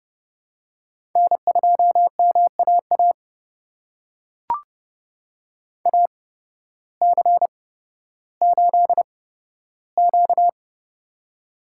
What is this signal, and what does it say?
Morse code, heard as a keyed tone, about 30 wpm, spaced as for 14 wpm, 700 Hz. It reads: N2MAA AC8Q
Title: USA Call Signs - 8x Character Spacing - 30wpm